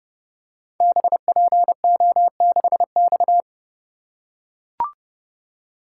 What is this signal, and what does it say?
Morse code, heard as a keyed tone, about 30 wpm, 700 Hz. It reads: BPO6X
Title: Code Groups - 5 Letter:Numbers - 30wpm